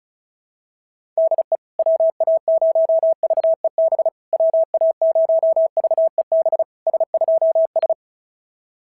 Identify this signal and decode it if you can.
Morse code, heard as a keyed tone, about 36 wpm, 690 Hz. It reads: DE WA0VEB WA0VEB S2S
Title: SOTA QSO Practice 35wpm